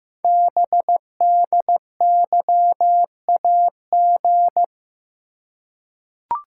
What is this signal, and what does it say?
Morse code, heard as a keyed tone, about 15 wpm, 700 Hz. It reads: BDYAG
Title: Code Groups - 5 Letters - 15wpm